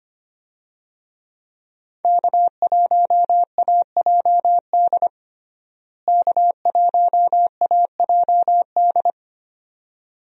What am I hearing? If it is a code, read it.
Morse code, heard as a keyed tone, about 25 wpm, 700 Hz. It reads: K1AJD K1AJD
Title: USA Call Signs - 5 Characters - Repeated 2x - 25wpm